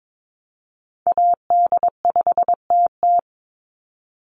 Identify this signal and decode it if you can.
Morse code, heard as a keyed tone, about 22 wpm, 700 Hz. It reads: AD5TT
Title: Call Signs Repeated 3x - 22wpm